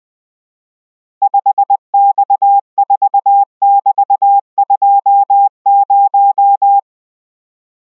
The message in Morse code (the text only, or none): 5X4<BT>20